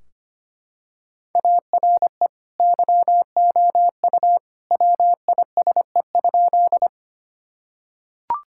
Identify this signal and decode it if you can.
Morse code, heard as a keyed tone, about 25 wpm, 700 Hz. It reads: ARE YOU WISE?
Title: Sentences from Top 1600 Words 25wpm